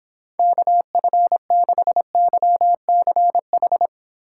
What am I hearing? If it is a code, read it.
Morse code, heard as a keyed tone, about 26 wpm, 700 Hz. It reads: KF6YCH